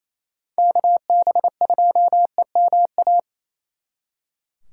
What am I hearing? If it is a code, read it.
Morse code, heard as a keyed tone, about 36 wpm, 700 Hz. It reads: KB2EMA